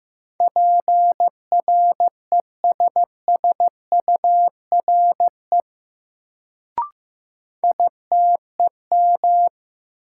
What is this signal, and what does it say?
Morse code, heard as a keyed tone, about 15 wpm, 700 Hz. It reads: PRESSURE ITEM
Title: Sentences from Top 1400 Words - New Word Drill Alt - 15wpm